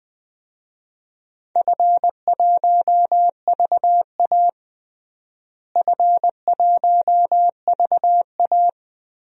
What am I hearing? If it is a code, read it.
Morse code, heard as a keyed tone, about 20 wpm, 700 Hz. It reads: F1VA F1VA
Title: DXCC-100 Call Signs - 4 Characters - Repeated 3x - 20wpm